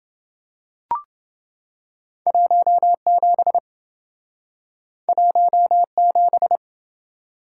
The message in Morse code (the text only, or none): 17 17